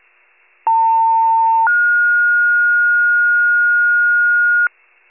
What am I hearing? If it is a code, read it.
Two-tone page, heard as a one-second tone, then a three-second tone, about 900 then 1430 Hz, with noise. It is heard from 0.7 to 4.7 s.
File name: pager / 78 FIRE / wcalert